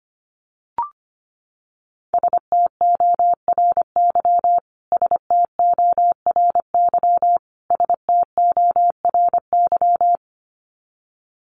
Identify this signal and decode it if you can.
Morse code, heard as a keyed tone, about 25 wpm, 700 Hz. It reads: STORY STORY STORY
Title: Top 300 Words Repeated 25wpm